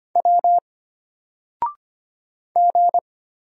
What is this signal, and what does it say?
Morse code, heard as a keyed tone, about 25 wpm, 700 Hz. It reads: W G
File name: Single-Letter-Number-25wpm